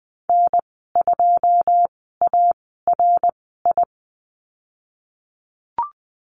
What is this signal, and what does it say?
Morse code, heard as a keyed tone, about 20 wpm, spaced as for 15 wpm, 700 Hz. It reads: N2ARI
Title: 082 Lesson 14g - Number 2 - Call Signs 20wpm